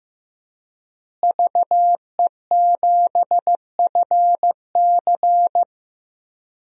Morse code, heard as a keyed tone, about 15 wpm, 700 Hz. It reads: VE7FC
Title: DXCC-100 Call Signs - 5 Characters - Repeated 2x - 15wpm